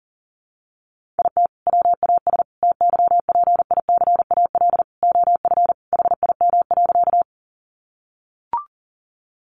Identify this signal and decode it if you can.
Morse code, heard as a keyed tone, about 40 wpm, 700 Hz. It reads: IT WAS TYPICAL OF HIM.
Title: Sentences from Top 1100 Words 40wpm